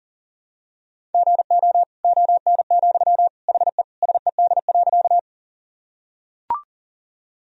Morse code, heard as a keyed tone, about 40 wpm, 700 Hz. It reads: GO ON, HE SED.
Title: Sentences from Top 100 Words and Abbreviations 40wpm